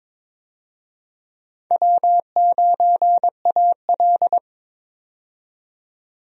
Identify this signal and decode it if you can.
Morse code, heard as a keyed tone, about 21 wpm, 700 Hz. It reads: W9AL
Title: USA Call Signs - 4 Characters - 22wpm